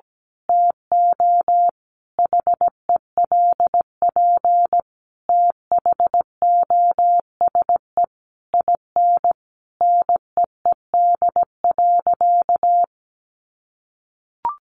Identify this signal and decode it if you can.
Morse code, heard as a keyed tone, about 17 wpm, 700 Hz. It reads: TO HELP THOSE IN NEED.
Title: Sentences from Top 1200 Words 17wpm